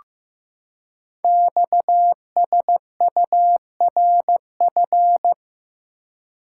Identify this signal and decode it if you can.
Morse code, heard as a keyed tone, about 15 wpm, 700 Hz. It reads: XSURF
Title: Code Groups - 5 Letters - 15wpm